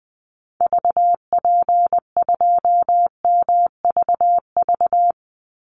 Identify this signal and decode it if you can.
Morse code, heard as a keyed tone, about 20 wpm, 700 Hz. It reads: VP2MVV
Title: Call Signs Repeated 3x - 20wpm